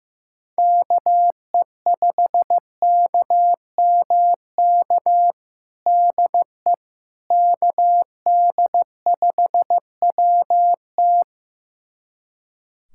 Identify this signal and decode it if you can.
Morse code, heard as a keyed tone, about 15 wpm, 700 Hz. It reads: KE5KMK DE KD5WT
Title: USA Callsign DE Callsign 15wpm